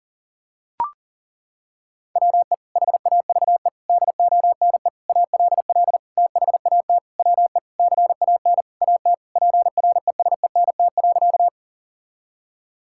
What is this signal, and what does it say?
Morse code, heard as a keyed tone, about 37 wpm, 700 Hz. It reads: WE HAVE DONE ALL THAT WE CAN AT PRESENT.
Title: Sentences from Top 1000 Words - Review Part 5 - 40wpm